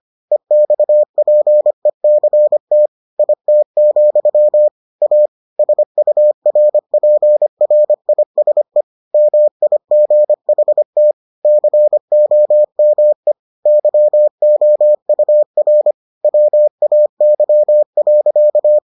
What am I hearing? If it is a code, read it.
Morse code, heard as a keyed tone, about 25 wpm, 600 Hz. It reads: EXPECT IT, A SURPRISE MIGHT COME YOUR WAY.